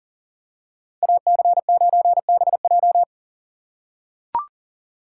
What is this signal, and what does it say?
Morse code, heard as a keyed tone, about 40 wpm, 700 Hz. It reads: AC9BJ
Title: USA Call Signs - 5 Characters - 4x Character Spacing - 40wpm